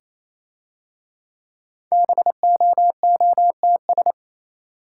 Morse code, heard as a keyed tone, about 27 wpm, 690 Hz. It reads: BOOTS
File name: Easy Words - Speed-Racing with Hints - 22wpm